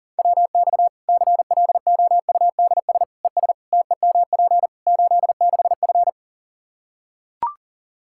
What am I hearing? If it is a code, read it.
Morse code, heard as a keyed tone, about 40 wpm, 700 Hz. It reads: WX CLOUDS ES TEMP 86F
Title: Practice QSO - Exchange 2 - Info - 40wpm